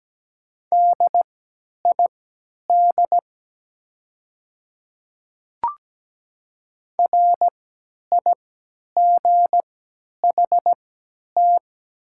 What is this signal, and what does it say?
Morse code, heard as a keyed tone, about 17 wpm, spaced as for 10 wpm, 700 Hz. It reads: DID RIGHT
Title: 3x Character Spacing - Top 300 Words - 17wpm